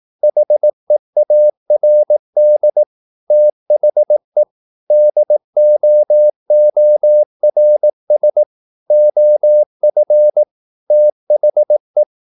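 Morse code, heard as a keyed tone, about 18 wpm, 600 Hz. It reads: HEARD THE DOORS OF THE
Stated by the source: Morse practice at 18 wpm, English text, from CW med Gnister.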